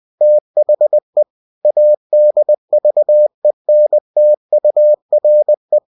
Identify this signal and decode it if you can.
Morse code, heard as a keyed tone, about 20 wpm, 600 Hz. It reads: THE ADVENTURE
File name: cw-books / winnie-the-pooh / winnie-20wpm-page-0063